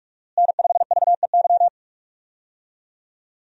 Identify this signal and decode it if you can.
Morse code, heard as a keyed tone, about 43 wpm, 700 Hz. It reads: N5VEY
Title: USA Call Signs - 5 Characters - Repeated 3x - 45wpm